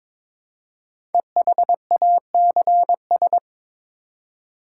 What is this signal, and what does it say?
Morse code, heard as a keyed tone, about 22 wpm, 700 Hz. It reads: EHACS
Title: Code Groups - 5 Letters - 22wpm